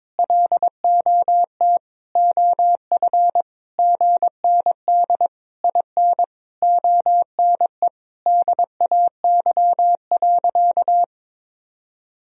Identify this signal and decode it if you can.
Morse code, heard as a keyed tone, about 22 wpm, 700 Hz. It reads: LOT OF GND IN ONE DAY.
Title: Sentences from Top 500 Words and Abbreviations 22wpm